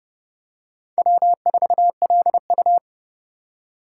Morse code, heard as a keyed tone, about 28 wpm, 700 Hz. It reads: W4LU